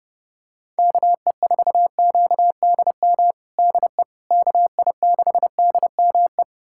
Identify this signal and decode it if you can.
Morse code, heard as a keyed tone, about 30 wpm, 700 Hz. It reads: KE4QDM DE KI6DME
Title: USA Callsign DE Callsign - 6 Characters - 30wpm